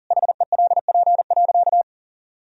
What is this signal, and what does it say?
Morse code, heard as a keyed tone, about 40 wpm, 700 Hz. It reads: HELP.
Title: Sentences from Top 200 Words and Abbreviations 40wpm